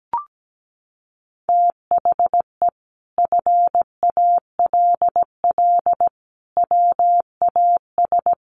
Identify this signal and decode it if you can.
Morse code, heard as a keyed tone, about 17 wpm, 700 Hz. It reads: THE FALL WAS
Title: Sentences from Top 1400 Words 17wpm